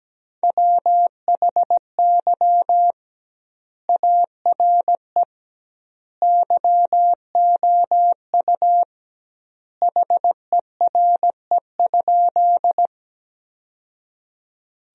Morse code, heard as a keyed tone, about 17 wpm, 700 Hz. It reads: WHY ARE YOU HERE?
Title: 2x Word-Spacing - Sentences from Top 200 Words - 17wpm